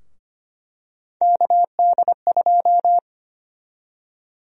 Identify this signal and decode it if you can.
Morse code, heard as a keyed tone, about 25 wpm, 700 Hz. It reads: KD2
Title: USA Call Sign Prefixes - Repeated 2x - 25wpm